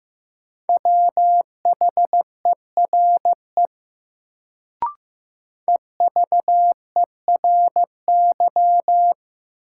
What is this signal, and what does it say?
Morse code, heard as a keyed tone, about 15 wpm, 700 Hz. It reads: WHERE EVERY
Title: Sentences from Top 200 Words - New Word Drill - 15wpm